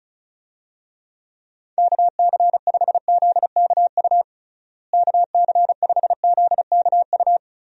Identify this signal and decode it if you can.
Morse code, heard as a keyed tone, about 34 wpm, 700 Hz. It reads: KC5ZKU KC5ZKU
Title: USA Call Signs - 6 Characters - Repeated 3x - 35wpm